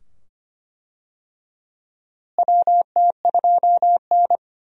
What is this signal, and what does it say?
Morse code, heard as a keyed tone, about 25 wpm, 700 Hz. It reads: WT2N
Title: USA Call Signs - 4 Characters - Repeated 2x - 25wpm